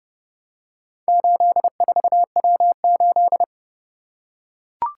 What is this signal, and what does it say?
Morse code, heard as a keyed tone, about 29 wpm, 700 Hz. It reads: 84W8